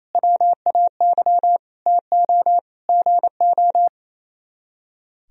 Morse code, heard as a keyed tone, about 34 wpm, 700 Hz. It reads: WAY TO GO
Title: Speed-Racing - Sets of 5 Words 28wpm